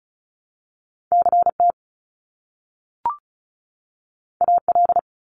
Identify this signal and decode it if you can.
Morse code, heard as a keyed tone, about 36 wpm, 700 Hz. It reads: CT AL
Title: ARRL Sections 35wpm